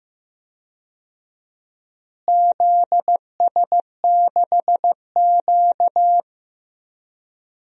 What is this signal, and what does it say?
Morse code, heard as a keyed tone, about 15 wpm, 700 Hz. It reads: ZS6Q
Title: DXCC-100 Call Signs - 4 Characters - Repeated 2x - 15wpm